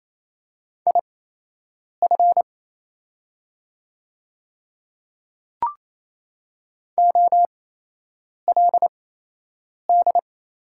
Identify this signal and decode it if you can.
Morse code, heard as a keyed tone, about 28 wpm, spaced as for 8 wpm, 700 Hz. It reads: IF OLD
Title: 8x Character Spacing - Top 200 Words - 28wpm